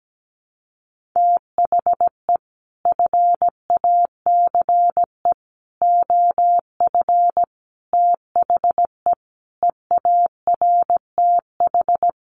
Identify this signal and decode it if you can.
Morse code, heard as a keyed tone, about 17 wpm, 700 Hz. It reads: THE FACE OF THE EARTH
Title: Sets of 5 Words Encore 17wpm